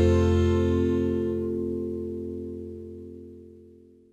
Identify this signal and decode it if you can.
Telephone call progress tone (North America), heard as a telephone network's tone, about 350, 440 Hz. Dial tone.